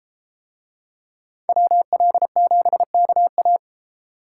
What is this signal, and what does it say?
Morse code, heard as a keyed tone, about 31 wpm, 700 Hz. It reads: WL7KA